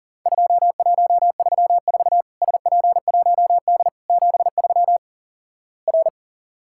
Morse code, heard as a keyed tone, about 37 wpm, 700 Hz. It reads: 2134 SP1D 73 R
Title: Worked All Europe DX Contest - QTC Exchange - 40wpm